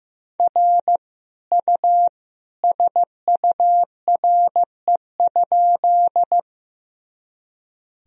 Morse code, heard as a keyed tone, about 15 wpm, 700 Hz. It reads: R U SURE?
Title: Sentences from Top 300 Words and Abbreviations 15wpm